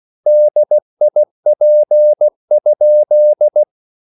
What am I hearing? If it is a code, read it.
Morse code, heard as a keyed tone, about 16 wpm, 600 Hz. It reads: DIP?